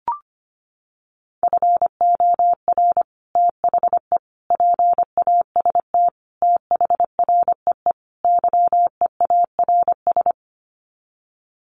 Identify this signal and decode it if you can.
Morse code, heard as a keyed tone, about 25 wpm, 700 Hz. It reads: FOR THE PAST THREE YEARS
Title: Sets of 5 Words Encore2 25wpm